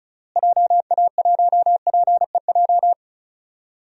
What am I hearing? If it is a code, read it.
Morse code, heard as a keyed tone, about 36 wpm, 700 Hz. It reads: JA1PEJ
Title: DXCC-100 Call Signs - 6 Characters - Repeated 3x - 35wpm